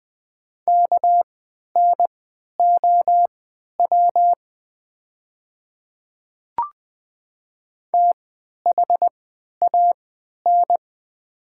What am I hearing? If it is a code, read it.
Morse code, heard as a keyed tone, about 20 wpm, spaced as for 11 wpm, 700 Hz. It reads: KNOW THAN
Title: 3x Character Spacing - Top 100 Words - 20wpm